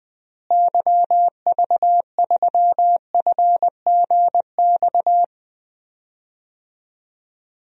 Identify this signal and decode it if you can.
Morse code, heard as a keyed tone, about 20 wpm, 700 Hz. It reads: YV3FGX
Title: DXCC-100 Call Signs - 6 Characters - Repeated 2x - 20wpm